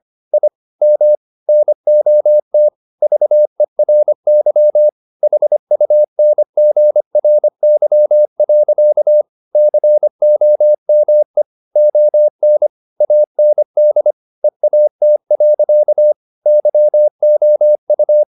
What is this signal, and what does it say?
Morse code, heard as a keyed tone, about 25 wpm, 600 Hz. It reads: I M NOT VERY HUNGRY. COME ON AND EAT. YOU
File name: the old man and the sea part 1 25wpm